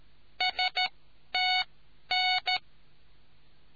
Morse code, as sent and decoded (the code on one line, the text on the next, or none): ... - -.
STN